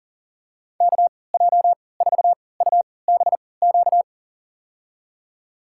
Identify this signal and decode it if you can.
Morse code, heard as a keyed tone, about 40 wpm, spaced as for 23 wpm, 700 Hz. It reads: KJ4UBQ
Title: USA Call Signs - 6 Characters - 3x Character Spacing - 40wpm